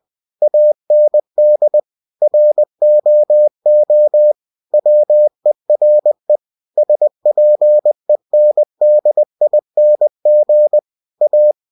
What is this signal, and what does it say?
Morse code, heard as a keyed tone, about 20 wpm, 600 Hz. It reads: AND ROO WERE SPENDING A
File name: cw-books / winnie-the-pooh / winnie-20wpm-page-0066